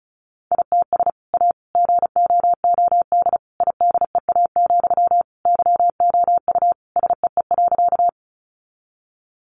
Morse code, heard as a keyed tone, about 34 wpm, 700 Hz. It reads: ITS A GOOD IDEA, YOU SEE.
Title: Sentences from Top 600 Words 35wpm